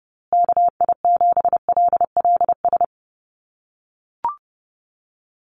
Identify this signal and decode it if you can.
Morse code, heard as a keyed tone, about 30 wpm, 700 Hz. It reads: KI7LLS
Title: USA Call Signs Repeated 3x 30wpm